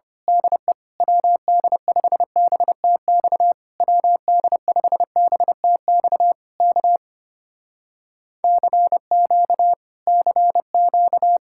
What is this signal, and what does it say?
Morse code, heard as a keyed tone, about 24 wpm, 700 Hz. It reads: DE WD5BTX WD5BTX K CQ CQ
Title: Speed-Racing Common QSO Phrases - 25wpm